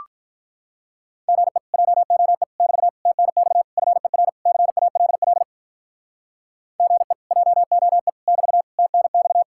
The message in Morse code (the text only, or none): GE JOE <BT> TNX FER CALL GE JOE <BT> TNX